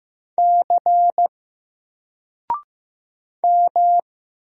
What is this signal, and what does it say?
Morse code, heard as a keyed tone, about 15 wpm, 700 Hz. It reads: C M
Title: Single Character (Letter:Number:Punctuation:Prosign) Alt 15wpm